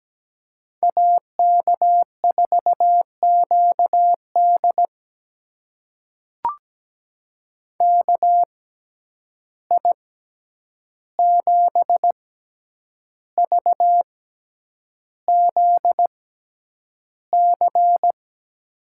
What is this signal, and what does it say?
Morse code, heard as a keyed tone, about 17 wpm, spaced as for 8 wpm, 700 Hz. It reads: AK4QD KI7VZC
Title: USA Call Signs - 6x Character Spacing - 17wpm